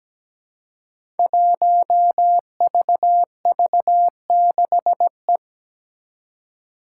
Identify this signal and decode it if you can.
Morse code, heard as a keyed tone, about 17 wpm, 700 Hz. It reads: 1VV6E
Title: Five Challenging Letter-Number Combinations 17wpm